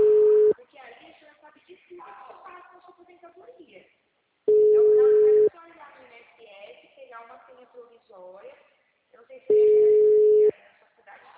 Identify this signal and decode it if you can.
Ringback tone (Europe, 425 Hz), with speech.
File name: out- -2002- - - .3873